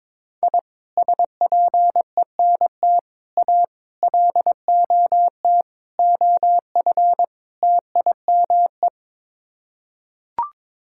Morse code, heard as a keyed tone, about 22 wpm, 700 Hz. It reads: I SPENT A LOT OF TIME
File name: practice / 22wpm / Sets of 6 Words - 22wpm